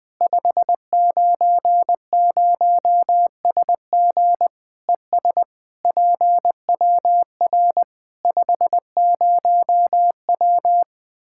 Morse code, heard as a keyed tone, about 20 wpm, 700 Hz. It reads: 590SG ES PWR 50W
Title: K1USN SST QSO Practice 20wpm